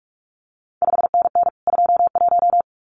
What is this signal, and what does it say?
Morse code, heard as a keyed tone, about 46 wpm, 700 Hz. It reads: 5NN 21
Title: CQ WW DX Contest Exchange Encore - 45wpm